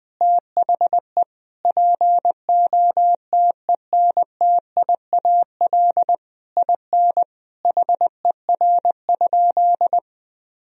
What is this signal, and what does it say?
Morse code, heard as a keyed tone, about 21 wpm, 700 Hz. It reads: THE POTENTIAL IN HER?
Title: Sentences from Top 1000 Words - Review Part 4 - 20wpm